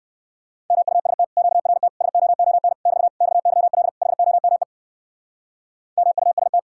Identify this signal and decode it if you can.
Morse code, heard as a keyed tone, about 65 wpm, 700 Hz. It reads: MUST QRT SOON <BT> XYL HOME MUST